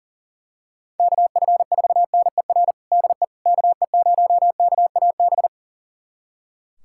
Morse code, heard as a keyed tone, about 40 wpm, 700 Hz. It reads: KF4NER DE KE0KAB
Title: USA Callsign DE Callsign 40wpm